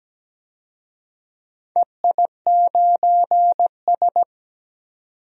Morse code, heard as a keyed tone, about 17 wpm, 700 Hz. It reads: EI9S
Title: DXCC-100 Call Signs - 4 Characters - 17wpm